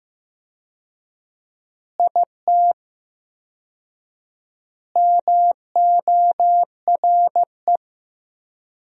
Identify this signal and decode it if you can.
Morse code, heard as a keyed tone, about 16 wpm, 700 Hz. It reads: IT MORE
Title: 4x Word-Spacing - Sentences from Top 400 Words - 15wpm